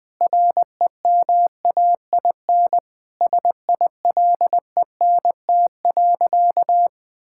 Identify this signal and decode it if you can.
Morse code, heard as a keyed tone, about 20 wpm, 700 Hz. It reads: REMAIN SILENT.
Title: Sentences with 4-Word Phrases - Spoken Beginning - 20wpm